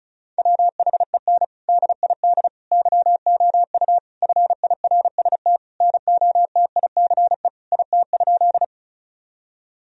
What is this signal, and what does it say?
Morse code, heard as a keyed tone, about 34 wpm, 700 Hz. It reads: WHEN DID YOU FIRST NOTICE IT?
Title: Sentences from Top 1000 Words - Review Part 2 - 35wpm